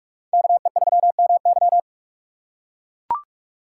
Morse code, heard as a keyed tone, about 44 wpm, 700 Hz. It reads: KE3MY